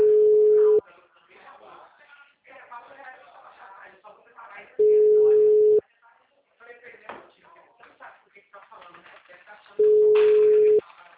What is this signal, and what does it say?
Ringback tone (Europe, 425 Hz), with speech.